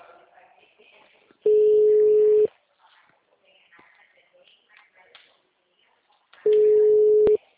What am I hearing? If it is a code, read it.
Ringback tone (Europe, 425 Hz), with speech.
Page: Itu - SP